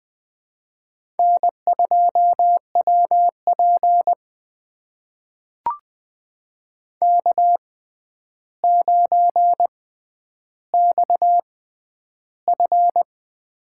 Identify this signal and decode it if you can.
Morse code, heard as a keyed tone, about 21 wpm, spaced as for 10 wpm, 700 Hz. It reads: N2WP K9XF
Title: USA Call Signs - 4 Characters - 6x Character Spacing - 20wpm